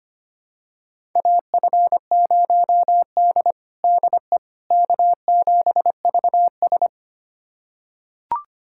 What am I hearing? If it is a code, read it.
Morse code, heard as a keyed tone, about 25 wpm, 700 Hz. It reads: AF0D DE K7VS
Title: USA Callsign DE Callsign - 4 Characters - 25wpm